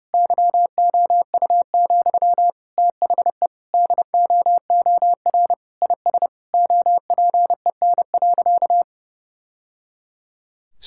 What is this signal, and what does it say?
Morse code, heard as a keyed tone, about 32 wpm, 700 Hz. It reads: YOU, THE DOOR IS OPEN.
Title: Sentences from Top 1000 Words - Review Part 6 - 30wpm